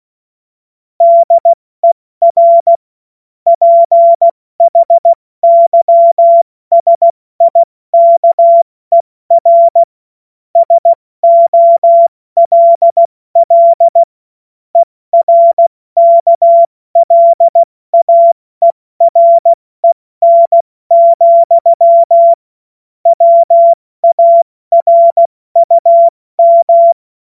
Morse code, heard as a keyed tone, about 16 wpm, spaced as for 14 wpm, 680 Hz. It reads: DER PHYSIKER SOLL ERKLAEREN, WARUM
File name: Morsekurs / Witze / Witz 21. Juni 2022 70 F80